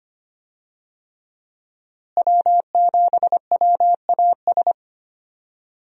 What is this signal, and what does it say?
Morse code, heard as a keyed tone, about 25 wpm, 700 Hz. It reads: W7WAS
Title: USA Call Signs - 5 Characters - Repeated 3x - 25wpm